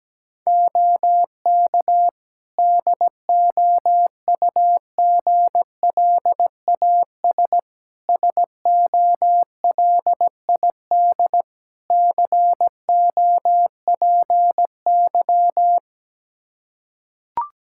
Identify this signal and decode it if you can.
Morse code, heard as a keyed tone, about 17 wpm, 700 Hz. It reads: OK DOUGLAS SOLID COPY
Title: Practice QSO - Exchange 3 - More Info - 17wpm